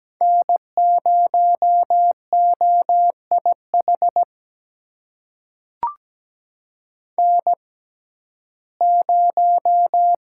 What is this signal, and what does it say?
Morse code, heard as a keyed tone, about 17 wpm, spaced as for 12 wpm, 700 Hz. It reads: N0OIH N0
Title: USA Call Signs - 5 Characters - 6x Character Spacing - 17wpm